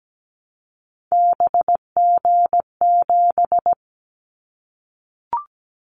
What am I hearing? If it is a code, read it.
Morse code, heard as a keyed tone, about 17 wpm, 700 Hz. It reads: BG7
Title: Callsign Prefixes for Top 100 DXCC - 17wpm